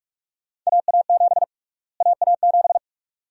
Morse code, heard as a keyed tone, about 46 wpm, 700 Hz. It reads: AA7 AA7
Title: USA Call Sign Prefixes - Repeated 2x - 45wpm